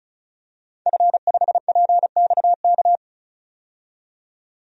Morse code, heard as a keyed tone, about 34 wpm, 700 Hz. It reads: F5PXK
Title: DXCC-100 Call Signs - 5 Characters - 35wpm